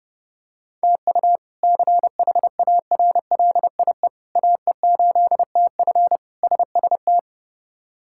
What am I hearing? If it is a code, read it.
Morse code, heard as a keyed tone, about 32 wpm, 700 Hz. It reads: TU CHARLIE AE8TF SST
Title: K1USN SST QSO Practice - 30wpm